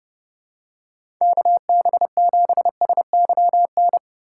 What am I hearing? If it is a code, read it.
Morse code, heard as a keyed tone, about 32 wpm, 700 Hz. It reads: KB7SYN
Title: USA Call Signs - 6 Characters - 2x Character Spacing - 30wpm